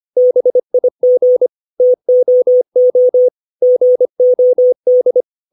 Morse code, heard as a keyed tone, about 25 wpm, 500 Hz. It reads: BIG TOO GOD
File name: 632 words / 632 Words 25WPM